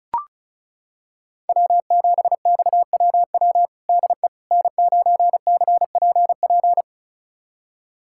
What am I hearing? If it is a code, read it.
Morse code, heard as a keyed tone, about 34 wpm, 690 Hz. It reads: W7XWW DE N9CPP
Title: USA Callsign DE Callsign - 5 Characters - 35wpm